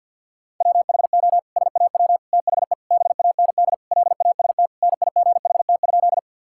Morse code, heard as a keyed tone, about 50 wpm, 700 Hz. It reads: WHO SAW THE BAND LAST NIGHT?
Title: Sentences from Top 1600 Words 50wpm